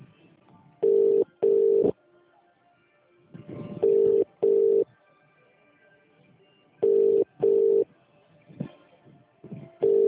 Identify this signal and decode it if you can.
Ringback tone (UK), 400 + 450 Hz.